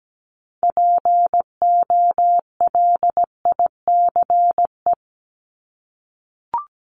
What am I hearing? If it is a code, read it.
Morse code, heard as a keyed tone, about 17 wpm, 700 Hz. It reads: POLICE
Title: Top 1751-2000 Words 17wpm